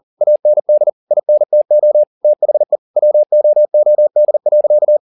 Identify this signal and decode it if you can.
Morse code, heard as a keyed tone, about 40 wpm, 600 Hz. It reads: AND INTO THE WOOD.
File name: the old man and the sea 2 40wpm